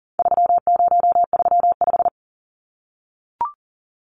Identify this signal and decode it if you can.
Morse code, heard as a keyed tone, about 40 wpm, 700 Hz. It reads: 3035
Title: Numbers - 4-Digit 40wpm